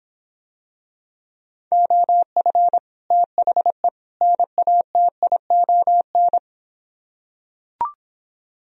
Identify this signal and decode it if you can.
Morse code, heard as a keyed tone, about 26 wpm, 700 Hz. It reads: OF THE NATION